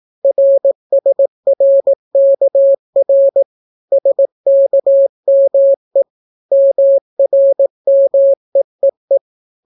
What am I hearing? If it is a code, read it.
Morse code, heard as a keyed tone, about 17 wpm, 550 Hz. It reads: RSRKR SKME MRMEEE